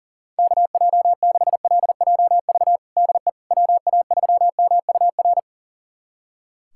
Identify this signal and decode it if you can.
Morse code, heard as a keyed tone, about 37 wpm, 700 Hz. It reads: KJ6LJV DE WA3MUR
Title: USA Callsign DE Callsign - 6 Characters - 40wpm